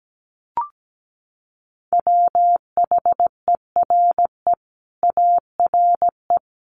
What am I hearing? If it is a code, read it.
Morse code, heard as a keyed tone, about 17 wpm, 700 Hz. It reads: WHERE ARE
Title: Sentences from Top 200 Words 17wpm